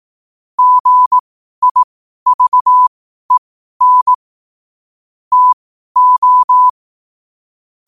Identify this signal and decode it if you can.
Morse code, heard as a keyed tone, about 18 wpm, spaced as for 12 wpm, 1000 Hz. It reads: GIVEN TO
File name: qotdfiles / 12wpm